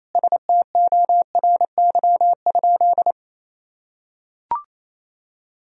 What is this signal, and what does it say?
Morse code, heard as a keyed tone, about 28 wpm, 700 Hz. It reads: STORY?
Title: Sentences from Top 1000 Words - Review Part 1 - 28wpm